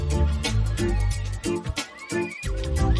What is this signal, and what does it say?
There is no code.